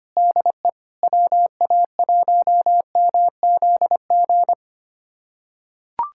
Morse code, heard as a keyed tone, about 25 wpm, 700 Hz. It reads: DE WA1MZG
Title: USA Callsign DE Callsign 25wpm